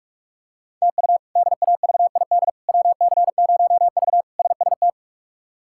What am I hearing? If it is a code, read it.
Morse code, heard as a keyed tone, about 46 wpm, 700 Hz. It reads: TU DAVID WC0V SST
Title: K1USN SST QSO Practice - 45wpm